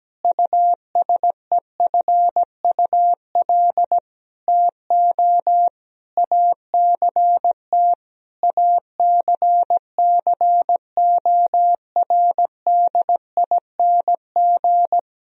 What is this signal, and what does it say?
Morse code, heard as a keyed tone, about 17 wpm, 700 Hz. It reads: USEFUL TO ACT ACCORDING
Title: Sentences from Top 1000 Words - Review Part 4 - 17wpm